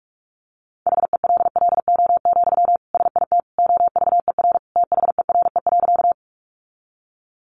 Morse code, heard as a keyed tone, about 44 wpm, 700 Hz. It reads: HELLO, SIT OVER THERE.
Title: Sentences from Top 1100 Words 45wpm